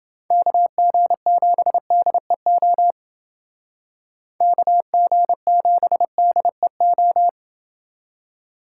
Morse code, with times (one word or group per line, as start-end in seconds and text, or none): 0.3-2.9 KG7DEO
4.4-7.3 KG7DEO